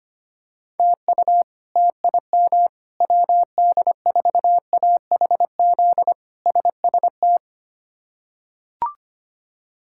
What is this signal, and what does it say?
Morse code, heard as a keyed tone, about 25 wpm, 700 Hz. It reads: TU TIM WD4AHZ SST
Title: K1USN SST QSO Practice - 25wpm